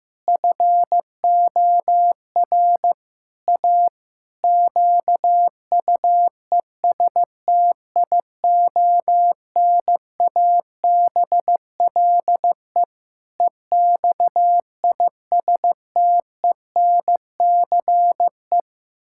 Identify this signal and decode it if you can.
Morse code, heard as a keyed tone, about 15 wpm, 700 Hz. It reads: FOR A QUESTIONABLE EXISTENCE
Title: Pangram Challenge v2 - 15wpm